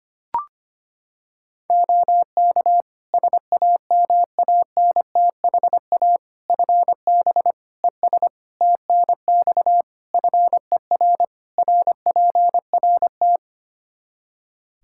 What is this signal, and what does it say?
Morse code, heard as a keyed tone, about 25 wpm, 700 Hz. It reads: OK SAMANTHA FB ES TNX FER RPRT
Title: Practice QSO - Exchange 2 - Info - 25wpm